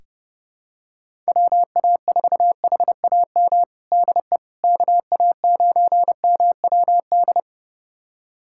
Morse code, heard as a keyed tone, about 30 wpm, 700 Hz. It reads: WA4HAM DE KA9MWD
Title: USA Callsign DE Callsign - 6 Characters - 30wpm